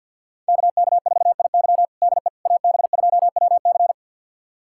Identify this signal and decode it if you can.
Morse code, heard as a keyed tone, about 53 wpm, 700 Hz. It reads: KK4IY DE AB2WC
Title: USA Callsign DE Callsign - 5 Characters - 50wpm